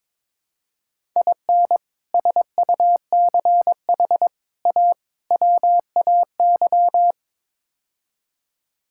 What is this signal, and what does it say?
Morse code, heard as a keyed tone, about 22 wpm, 700 Hz. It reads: IN SUCH A WAY
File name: practice / 22wpm / Build A Phrase - 5-Word Phrases - 22wpm